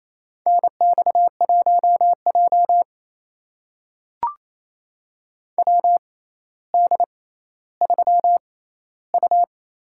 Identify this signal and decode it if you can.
Morse code, heard as a keyed tone, about 27 wpm, spaced as for 14 wpm, 700 Hz. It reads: NX1J WD3U
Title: USA Call Signs - 4 Characters - 6x Character Spacing - 28wpm